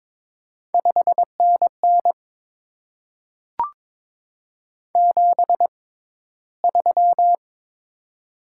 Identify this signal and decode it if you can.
Morse code, heard as a keyed tone, about 22 wpm, spaced as for 14 wpm, 700 Hz. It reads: 5NN 73
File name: Top 20 QSO Elements - 6x Character Spacing - 22wpm